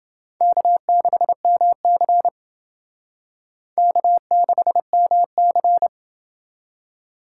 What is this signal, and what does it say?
Morse code, heard as a keyed tone, about 29 wpm, 700 Hz. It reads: K6MC K6MC